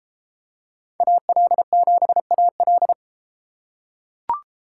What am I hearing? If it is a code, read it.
Morse code, heard as a keyed tone, about 33 wpm, 700 Hz. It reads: AL7AL